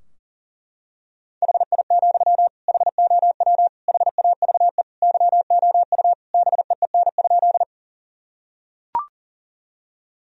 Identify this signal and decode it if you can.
Morse code, heard as a keyed tone, about 40 wpm, 700 Hz. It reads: HI, HOW HAVE YOU BEEN?
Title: Sentences from Top 1000 Words - Review Part 3 - 40wpm